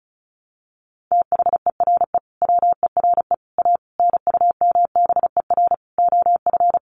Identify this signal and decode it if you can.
Morse code, heard as a keyed tone, about 34 wpm, 700 Hz. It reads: THERE WERE A NUMBER OF
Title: Sets of 5 Words Encore2 35wpm